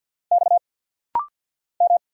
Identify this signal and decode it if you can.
Morse code, heard as a keyed tone, about 50 wpm, 690 Hz. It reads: X M
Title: Single Letter-Number Alt - Rapid-Fire 50wpm